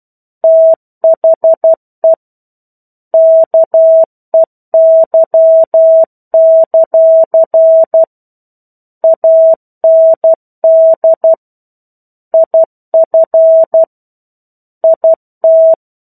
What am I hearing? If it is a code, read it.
Morse code, heard as a keyed tone, about 12 wpm, 650 Hz. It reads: THE KEY; AND IF IT
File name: 01-Chapter-01 12 WPM